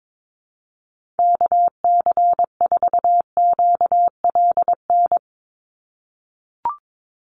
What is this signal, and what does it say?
Morse code, heard as a keyed tone, about 22 wpm, 700 Hz. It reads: KC4QLN
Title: USA Call Signs 22wpm